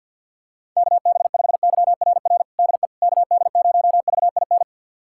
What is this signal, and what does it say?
Morse code, heard as a keyed tone, about 53 wpm, 700 Hz. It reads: KB5YRR DE KD0VIN
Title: USA Callsign DE Callsign 50wpm